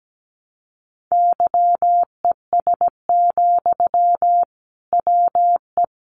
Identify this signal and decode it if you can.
Morse code, heard as a keyed tone, about 17 wpm, 700 Hz. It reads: YES, WE
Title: Sentences from Top 500 Words 17wpm